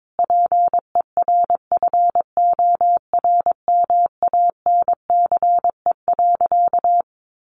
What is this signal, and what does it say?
Morse code, heard as a keyed tone, about 22 wpm, 700 Hz. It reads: PERFORMANCE.
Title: Sentences from Top 1100 Words 22wpm